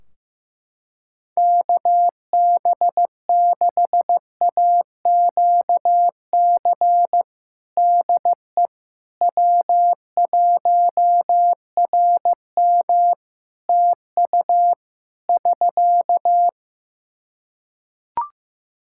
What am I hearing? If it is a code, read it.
Morse code, heard as a keyed tone, about 16 wpm, 700 Hz. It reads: KB6AQC DE W1RM TU <SK>
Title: Practice QSO - Exchange 4 - Ending - 15wpm